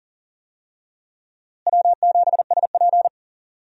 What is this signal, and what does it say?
Morse code, heard as a keyed tone, about 38 wpm, 700 Hz. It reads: W7SP
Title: USA Call Signs - 4 Characters - Repeated 3x - 40wpm